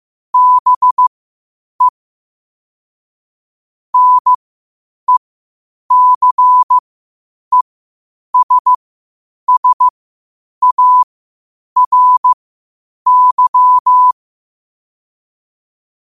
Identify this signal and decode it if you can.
Morse code, heard as a keyed tone, about 15 wpm, spaced as for 8 wpm, 1000 Hz. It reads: BE NECESSARY
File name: qotdfiles / 8wpm